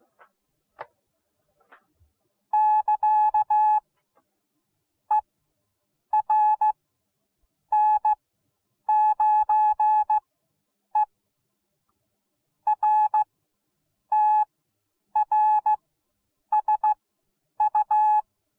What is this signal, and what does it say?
Morse code, heard as a keyed tone, about 16 wpm, spaced as for 8 wpm, 850 Hz. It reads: <KA> ERN9E RTRSU